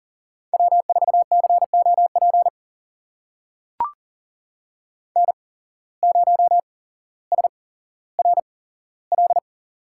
Morse code, heard as a keyed tone, about 40 wpm, spaced as for 17 wpm, 700 Hz. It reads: W4COP N0SRL
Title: USA Call Signs - 5 Characters - 8x Character Spacing - 40wpm